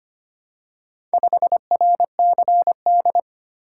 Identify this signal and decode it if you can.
Morse code, heard as a keyed tone, about 25 wpm, 700 Hz. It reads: 5RCD
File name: practice / 25wpm / Four Letter-Number Combinations - 25wpm